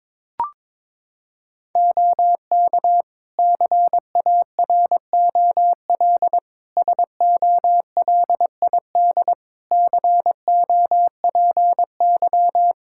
Morse code, heard as a keyed tone, about 21 wpm, 700 Hz. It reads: OK CAROL SOLID COPY
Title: K1USN SST QSO Practice 22wpm